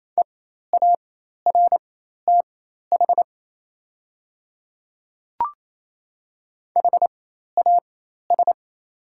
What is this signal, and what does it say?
Morse code, heard as a keyed tone, about 28 wpm, spaced as for 13 wpm, 700 Hz. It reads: EARTH HAS
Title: 4x Character Spacing - Top 300 Words - 28wpm